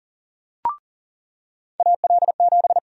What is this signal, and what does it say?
Morse code, heard as a keyed tone, about 40 wpm, 700 Hz. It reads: AL7
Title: USA Call Sign Prefixes - 40wpm